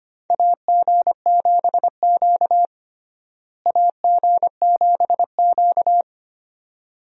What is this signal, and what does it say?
Morse code, heard as a keyed tone, about 25 wpm, 700 Hz. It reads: AG7Q AG7Q
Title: USA Call Signs - 4 Characters - Repeated 2x - 25wpm